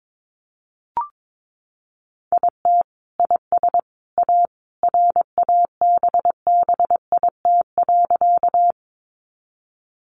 Morse code, heard as a keyed tone, about 22 wpm, 700 Hz. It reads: IT IS A RABBIT.
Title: Easy Phrases 22wpm